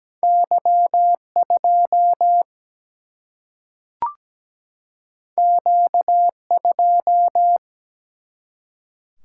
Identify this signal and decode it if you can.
Morse code, heard as a keyed tone, about 17 wpm, 700 Hz. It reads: Y 2 Q 2
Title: Single-Letter Single-Number 17wpm